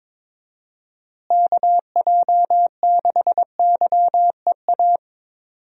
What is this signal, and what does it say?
Morse code, heard as a keyed tone, about 22 wpm, 700 Hz. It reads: KJ6YEA
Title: USA Call Signs - 6 Characters - Repeated 2x - 22wpm